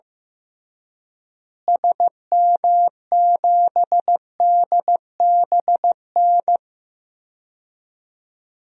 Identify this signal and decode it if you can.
Morse code, heard as a keyed tone, about 15 wpm, 700 Hz. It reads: SM7DBN
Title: DXCC-100 Call Signs - 6 Characters - Repeated 3x - 15wpm